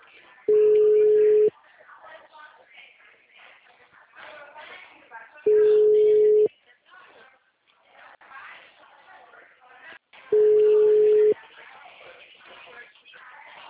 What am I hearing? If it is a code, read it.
Ringback tone (Europe, 425 Hz), with speech.
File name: out- -2041- - - .20814